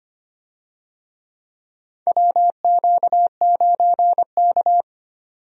Morse code, heard as a keyed tone, about 25 wpm, 700 Hz. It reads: WQ9K
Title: USA Call Signs - 4 Characters - 25wpm